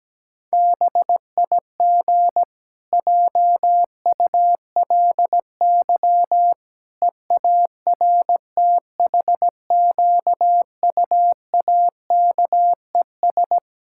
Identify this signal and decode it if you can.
Morse code, heard as a keyed tone, about 17 wpm, 700 Hz. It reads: BIG JULY EARTHQUAKES
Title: Pangram Challenge v2 - 17wpm